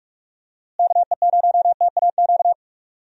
Morse code, heard as a keyed tone, about 42 wpm, 700 Hz. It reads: KE0TAQ
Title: USA Call Signs - 6 Characters - Repeated 2x - 45wpm